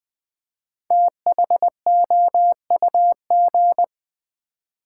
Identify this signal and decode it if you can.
Morse code, heard as a keyed tone, about 20 wpm, 700 Hz. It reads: THOUG
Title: Build A Word - 5 Letters At A Time - 10-Letter Words - 20wpm